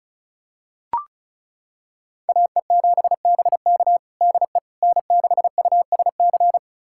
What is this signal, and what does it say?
Morse code, heard as a keyed tone, about 34 wpm, 700 Hz. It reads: AE7BK DE N6USC
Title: USA Callsign DE Callsign - 5 Characters - 35wpm